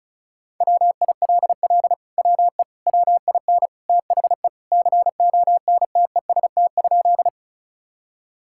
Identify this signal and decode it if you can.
Morse code, heard as a keyed tone, about 34 wpm, 700 Hz. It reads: WILL WE WIN THE CONTEST?
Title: Sentences from Top 1600 Words 35wpm